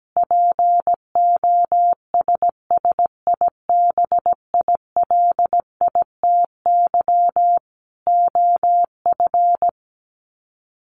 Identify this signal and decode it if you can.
Morse code, heard as a keyed tone, about 17 wpm, 700 Hz. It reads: POSSIBILITY OF
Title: Sets of 3 Words 17wpm